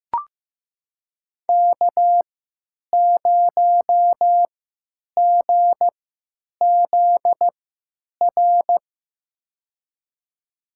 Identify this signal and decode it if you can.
Morse code, heard as a keyed tone, about 15 wpm, spaced as for 9 wpm, 700 Hz. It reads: K0GZR
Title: USA Call Signs - 5 Characters - 3x Character Spacing - 15wpm